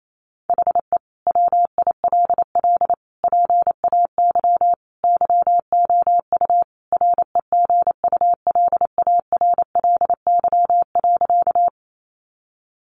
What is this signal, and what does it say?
Morse code, heard as a keyed tone, about 28 wpm, 700 Hz. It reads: HE WILL PAY YOU REGULARLY.
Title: Sentences from Top 1400 Words 28wpm